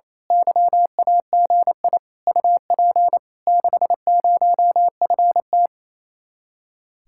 Morse code, heard as a keyed tone, about 26 wpm, 700 Hz. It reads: YAGI UP 60FT
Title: Speed-Racing Common QSO Phrases - 28wpm